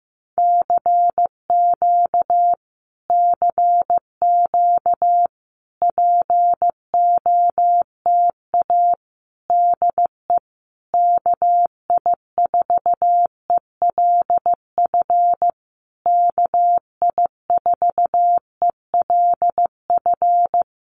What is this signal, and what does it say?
Morse code, heard as a keyed tone, about 15 wpm, 700 Hz. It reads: CQ CQ POTA DE KI4ELF KI4ELF
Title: Common QSO Phrases Encore 15wpm